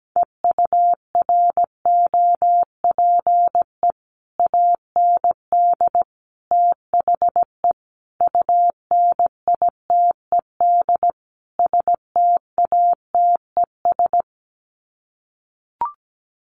Morse code, heard as a keyed tone, about 17 wpm, 700 Hz. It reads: EUROPE AND THE UNITED STATES
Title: Sets of 5 Words Encore2 17wpm